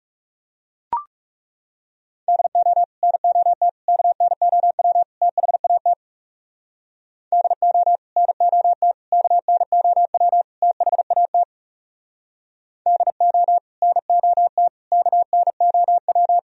Morse code, heard as a keyed tone, about 37 wpm, 700 Hz. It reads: DO NOT KNOW THAT DO NOT KNOW THAT DO NOT KNOW
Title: Speed-Racing - Sets of 4 Words 30wpm